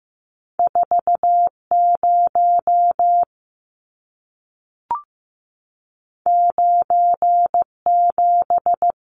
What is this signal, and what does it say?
Morse code, heard as a keyed tone, about 15 wpm, 700 Hz. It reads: 40 97
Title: Numbers - 2-Digit 15wpm